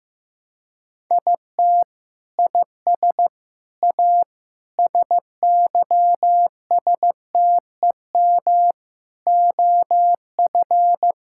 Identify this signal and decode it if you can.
Morse code, heard as a keyed tone, about 16 wpm, 700 Hz. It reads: IT IS A SYSTEM OF
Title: Sentences from Top 1000 Words - Review Part 6 - 15wpm